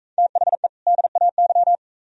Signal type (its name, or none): Morse code